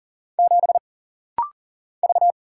Morse code, heard as a keyed tone, about 39 wpm, 700 Hz. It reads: 7 V
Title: Single Letter-Number - Rapid-Fire 40wpm